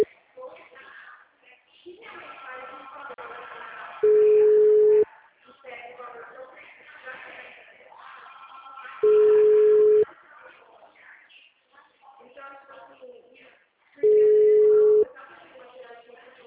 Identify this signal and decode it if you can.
Ringback tone (Europe, 425 Hz), with speech.